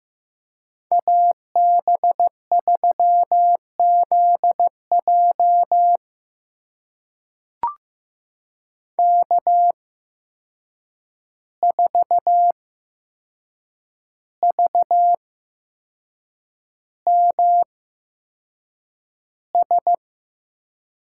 Morse code, heard as a keyed tone, about 15 wpm, spaced as for 6 wpm, 700 Hz. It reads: AB3ZJ K4VMS
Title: USA Call Signs - 5 Characters - 8x Character Spacing - 15wpm